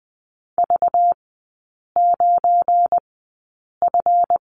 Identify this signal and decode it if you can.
Morse code, heard as a keyed tone, about 20 wpm, spaced as for 8 wpm, 700 Hz. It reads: V9F
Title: 198 Lesson 32e - Letter k - Three Characters 20wpm